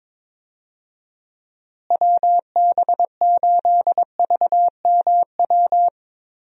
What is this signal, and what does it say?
Morse code, heard as a keyed tone, about 22 wpm, 700 Hz. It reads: WB8VMW
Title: USA Call Signs - 6 Characters - Repeated 2x - 22wpm